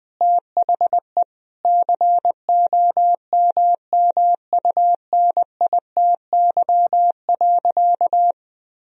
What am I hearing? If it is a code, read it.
Morse code, heard as a keyed tone, about 20 wpm, 700 Hz. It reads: THE COMMUNITY.
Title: Sentences from Top 1000 Words - Review Part 4 - 20wpm